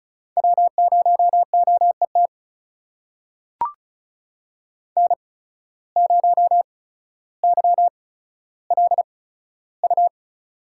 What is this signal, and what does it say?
Morse code, heard as a keyed tone, about 34 wpm, spaced as for 15 wpm, 700 Hz. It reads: W0OET N0YLU
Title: USA Call Signs - 5 Characters - 8x Character Spacing - 35wpm